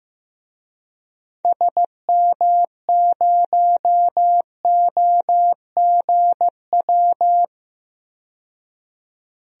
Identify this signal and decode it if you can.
Morse code, heard as a keyed tone, about 15 wpm, 700 Hz. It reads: SM0OGW
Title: DXCC-100 Call Signs - 6 Characters - 15wpm